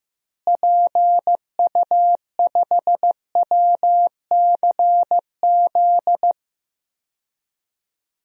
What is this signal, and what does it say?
Morse code, heard as a keyed tone, about 16 wpm, 700 Hz. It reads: PU5WCZ
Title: DXCC-100 Call Signs - 6 Characters - Repeated 3x - 15wpm